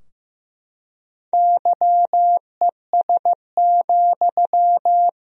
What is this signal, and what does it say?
Morse code, heard as a keyed tone, about 14 wpm, 700 Hz. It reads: YES,
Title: 4x Word-Spacing - Sentences from Top 500 Words - 15wpm